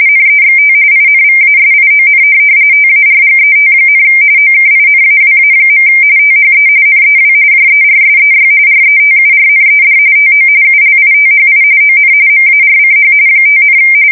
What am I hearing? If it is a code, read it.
RTTY, about 45.45 baud. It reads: NEXT INVASION TARGET, OKINAWA. THE NEW JERSEY RETURNED TO ULITHI ON MARCH 5,
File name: fsk170 / USS-New-Jersey